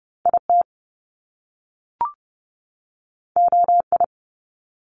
Morse code, heard as a keyed tone, about 30 wpm, 700 Hz. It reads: IT OI
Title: Top Two Letter Combinations in Words 30wpm